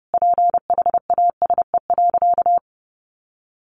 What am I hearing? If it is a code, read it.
Morse code, heard as a keyed tone, about 30 wpm, 700 Hz. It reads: PHASE.
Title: Sentences from Top 1200 Words 30wpm